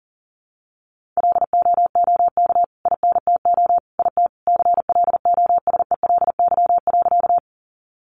Morse code, heard as a keyed tone, about 40 wpm, 700 Hz. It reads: LOOK INTO IT CLOSELY.
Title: Sentences from Top 1400 Words 40wpm